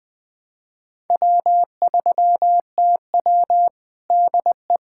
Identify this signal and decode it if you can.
Morse code, heard as a keyed tone, about 20 wpm, 700 Hz. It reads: W3TW DE
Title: Practice QSO - Exchange 4 - Ending - 20wpm